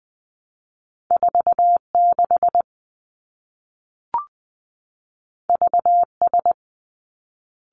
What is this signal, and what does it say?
Morse code, heard as a keyed tone, about 20 wpm, 700 Hz. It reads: 4 6 V S
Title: Two Challenging Letter-Number Combinations 20wpm